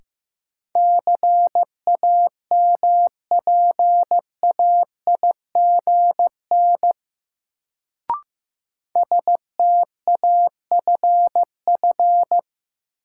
Morse code, heard as a keyed tone, about 15 wpm, 700 Hz. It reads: CAMPAIGN STAFF
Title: Sentences from Top 1200 Words - New Word Drill Alt - 15wpm